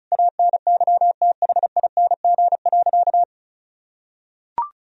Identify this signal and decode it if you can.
Morse code, heard as a keyed tone, about 36 wpm, 700 Hz. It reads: ANYTHING.
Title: Sentences from Top 1000 Words - Review Part 4 - 35wpm